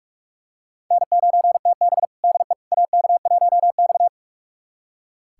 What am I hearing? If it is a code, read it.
Morse code, heard as a keyed tone, about 41 wpm, 700 Hz. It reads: N9TB DE AK1X
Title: USA Callsign DE Callsign - 4 Characters - 45wpm